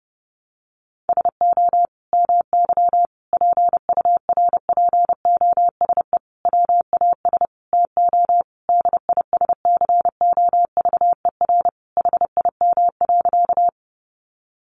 Morse code, heard as a keyed tone, about 30 wpm, 700 Hz. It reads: SO MY PURPOSE WAS TO DISCOVER HIM.
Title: Sentences from Top 1100 Words 30wpm